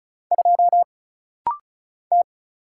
Morse code, heard as a keyed tone, about 33 wpm, 700 Hz. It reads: 2 T
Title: Single Letter-Number Alt - Rapid-Fire 35wpm